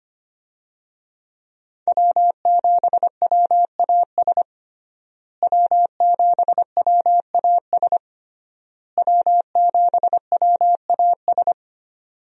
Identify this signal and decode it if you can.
Morse code, heard as a keyed tone, about 25 wpm, 700 Hz. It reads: W7WAS W7WAS W7WAS
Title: USA Call Signs - 5 Characters - Repeated 3x - 25wpm